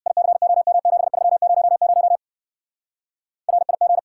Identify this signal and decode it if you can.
Morse code, heard as a keyed tone, about 66 wpm, 700 Hz. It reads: ICOM7300 RIG